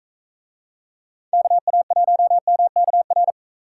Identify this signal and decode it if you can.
Morse code, heard as a keyed tone, about 41 wpm, 700 Hz. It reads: KA1MKR